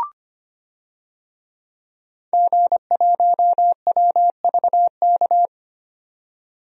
Morse code, heard as a keyed tone, about 25 wpm, 700 Hz. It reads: G1WVK
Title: DXCC-100 Call Signs - 5 Characters - 25wpm